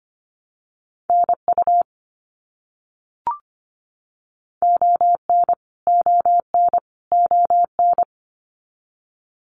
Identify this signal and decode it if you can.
Morse code, heard as a keyed tone, about 25 wpm, 700 Hz. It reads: NU ON ON ON
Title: Canadian Provinces and Territories - Repeated 3x - 25wpm